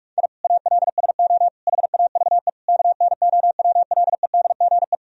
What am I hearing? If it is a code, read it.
Morse code, heard as a keyed tone, about 46 wpm, 700 Hz. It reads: I ALSO HAVE KNOWLEDGE
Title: Sentences from Top 1000 Words - Review Part 3 - 45wpm